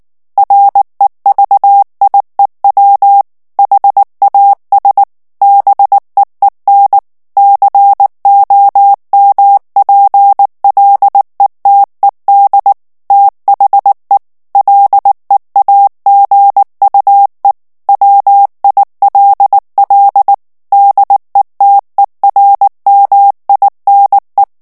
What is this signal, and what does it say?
Morse code, heard as a keyed tone, about 19 wpm, 800 Hz. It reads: REVIEW HAS BEEN COMPLETED THE LEAGUE WILL DETERMINE
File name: ANGLAIS texte clair 3 vitesse 18 mots minute